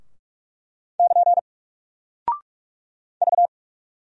Morse code, heard as a keyed tone, about 46 wpm, 700 Hz. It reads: <KN> V
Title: Single Character (Letter:Number:Punctuation:Prosign) Alt 45wpm